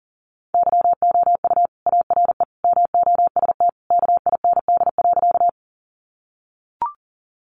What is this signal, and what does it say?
Morse code, heard as a keyed tone, about 40 wpm, 700 Hz. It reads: YOU ARE MOST KIND.
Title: Sentences from Top 200 Words 40wpm